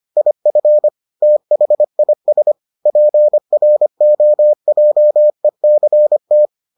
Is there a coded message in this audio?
Yes: Morse code from 0.2 to 6.5 s, heard as a keyed tone, about 25 wpm, 600 Hz. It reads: IF THIS PROJECT